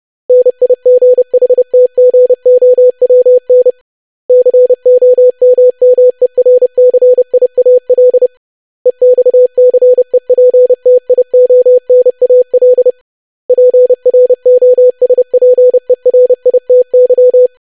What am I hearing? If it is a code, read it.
Morse code, heard as a keyed tone, about 30 wpm, 500 Hz. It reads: NIGHTGOWN COMMERCIAL EXCEPTIONAL PROSPERITY
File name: lesson / web / longwords-25wpm